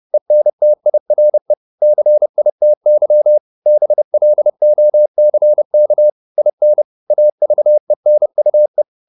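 Morse code, heard as a keyed tone, about 30 wpm, 600 Hz. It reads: ENTIRE CITY BLOCK IN AVENUE
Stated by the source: Morse code at 29 wpm, sending Ken Follett's novel Never in English, from CW med Gnister.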